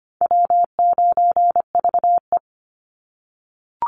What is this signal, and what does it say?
Morse code, heard as a keyed tone, about 25 wpm, 700 Hz. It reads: W9VE
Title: Easy Call Signs - 25wpm